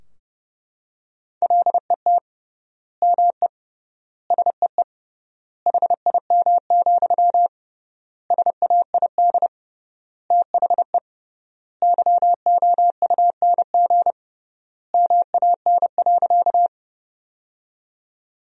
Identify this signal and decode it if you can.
Morse code, heard as a keyed tone, about 32 wpm, 700 Hz. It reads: LET ME SEE HIM, SAID THE YOUNG MAN.
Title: 3x Word-Spacing - Sentences from Top 400 Words - 30wpm